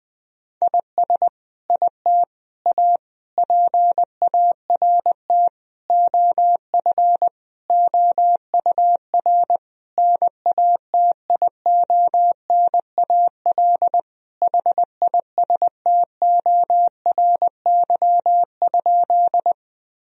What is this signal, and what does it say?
Morse code, heard as a keyed tone, about 20 wpm, 700 Hz. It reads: IS IT A PART OF OUR NATIONAL HISTORY?
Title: Sentences from Top 1000 Words - Review Part 4 - 20wpm